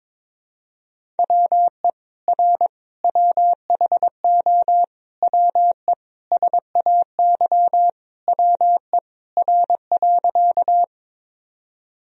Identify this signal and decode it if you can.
Morse code, heard as a keyed tone, about 22 wpm, 700 Hz. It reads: WE R WHO WE SAY WE R.
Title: Sentences from Top 200 Words and Abbreviations 22wpm